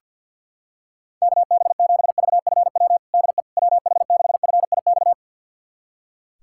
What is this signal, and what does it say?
Morse code, heard as a keyed tone, about 45 wpm, 690 Hz. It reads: KB7VFW DE WH6FIX
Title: USA Callsign DE Callsign - 6 Characters - 50wpm